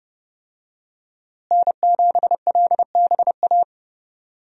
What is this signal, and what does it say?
Morse code, heard as a keyed tone, about 30 wpm, 700 Hz. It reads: N7LBA
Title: USA Call Signs - 5 Characters - Repeated 2x - 30wpm